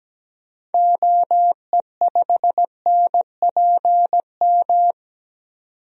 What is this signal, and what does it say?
Morse code, heard as a keyed tone, about 17 wpm, 700 Hz. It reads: OE5NPM
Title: DXCC-100 Call Signs - 6 Characters - 17wpm